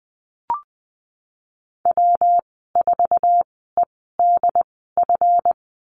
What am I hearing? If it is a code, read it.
Morse code, heard as a keyed tone, about 20 wpm, spaced as for 15 wpm, 700 Hz. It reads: W4EDF
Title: 154 Lesson 25g - Letter p - Call Signs 20wpm